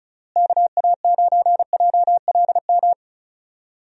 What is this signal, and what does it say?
Morse code, heard as a keyed tone, about 36 wpm, 700 Hz. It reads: KA9JLM
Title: USA Call Signs - 6 Characters - Repeated 2x - 35wpm